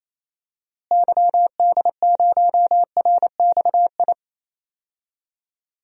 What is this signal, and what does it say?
Morse code, heard as a keyed tone, about 28 wpm, 700 Hz. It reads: YD0RXI
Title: DXCC-100 Call Signs - 6 Characters - Repeated 2x - 28wpm